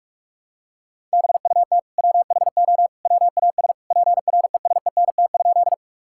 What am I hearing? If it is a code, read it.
Morse code, heard as a keyed tone, about 46 wpm, 700 Hz. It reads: BUT WHO WAS PRESENT?
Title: Sentences from Top 1000 Words - Review Part 5 - 45wpm